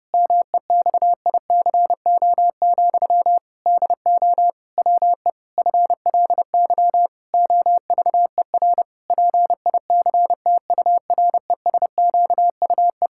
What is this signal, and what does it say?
Morse code, heard as a keyed tone, about 30 wpm, 700 Hz. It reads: MEXICO, DO WE FLY OVER PICTURESQUE
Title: Pangram Challenge v2 - 30wpm